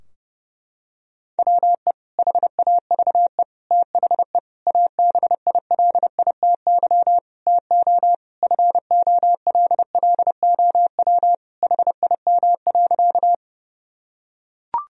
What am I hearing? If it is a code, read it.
Morse code, heard as a keyed tone, about 32 wpm, 700 Hz. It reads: WE HAVE THE ABILITY TO FOLLOW HIM.
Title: Sentences from Top 1000 Words - Review Part 3 - 30wpm